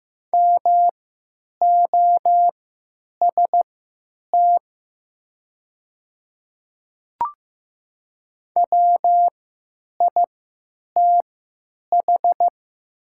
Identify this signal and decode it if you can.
Morse code, heard as a keyed tone, about 15 wpm, spaced as for 9 wpm, 700 Hz. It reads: MOST WITH
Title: 3x Character Spacing - Top 100 Words - 15wpm